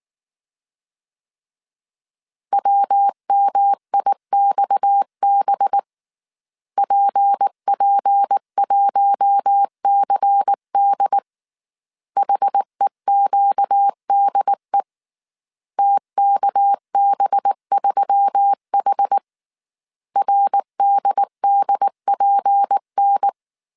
Morse code, heard as a keyed tone, about 19 wpm, 800 Hz. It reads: WMIXB PP1CD HEQDE TKB3H RDDPN
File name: CW / DET / 1 6 1500C S A 18wpm 2255